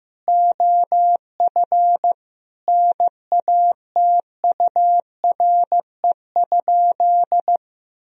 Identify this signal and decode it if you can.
Morse code, heard as a keyed tone, about 15 wpm, 700 Hz. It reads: OF NATURE?
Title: Sentences from Top 1000 Words - Review Part 3 - 15wpm